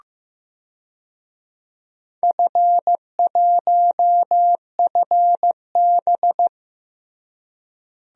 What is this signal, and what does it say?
Morse code, heard as a keyed tone, about 15 wpm, 700 Hz. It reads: F1FB
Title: DXCC-100 Call Signs - 4 Characters - Repeated 3x - 15wpm